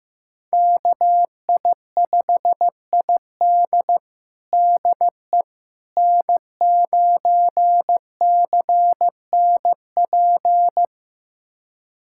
Morse code, heard as a keyed tone, about 15 wpm, 700 Hz. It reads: KI5ID DE N9CNP
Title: USA Callsign DE Callsign - 5 Characters - 15wpm